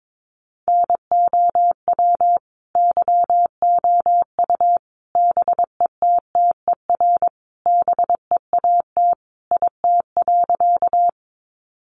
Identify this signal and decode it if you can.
Morse code, heard as a keyed tone, about 22 wpm, 700 Hz. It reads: NOW YOU BETTER BEAT IT.
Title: Sentences from Top 1300 Words 22wpm